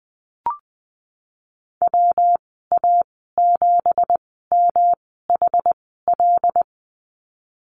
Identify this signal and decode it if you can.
Morse code, heard as a keyed tone, about 20 wpm, spaced as for 15 wpm, 700 Hz. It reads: WA7MHL
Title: 213 Lesson 34g - Number 8 - Call Signs 20wpm